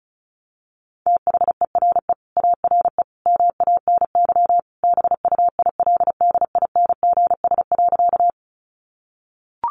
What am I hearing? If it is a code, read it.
Morse code, heard as a keyed tone, about 34 wpm, 700 Hz. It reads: THERE ARE MANY BUILDINGS.
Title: Easy Phrases 35wpm